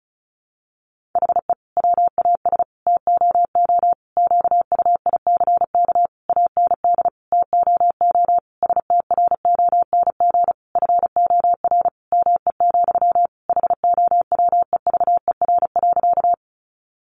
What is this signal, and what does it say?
Morse code, heard as a keyed tone, about 34 wpm, 700 Hz. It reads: HE WAS TOO QUICK AND TOO STRONG FOR ME, HOWEVER.
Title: Sentences from Top 600 Words 35wpm